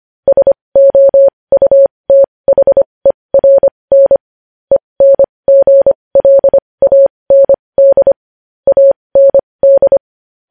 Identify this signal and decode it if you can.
Morse code, heard as a keyed tone, about 25 wpm, spaced as for 20 wpm, 550 Hz. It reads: SOUTHERN ENGLAND AND
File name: podcast / News headlines-1x-25x20